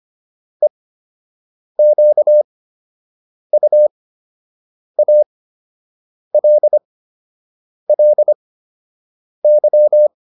Morse code, heard as a keyed tone, about 25 wpm, spaced as for 7 wpm, 600 Hz. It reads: EQUALLY